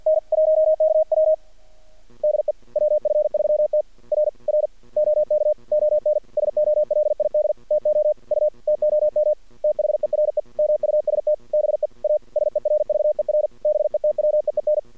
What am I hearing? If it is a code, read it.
Morse code, heard as a keyed tone, about 48 wpm, 610 Hz. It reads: N 1KW DE WF3T R R OK ON SOLID TO U TOM THERE MUST BE A SERVER BETWEEN
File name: archived / CRSnet